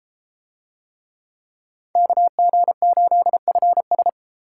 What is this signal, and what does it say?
Morse code, heard as a keyed tone, about 31 wpm, 700 Hz. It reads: KG8FS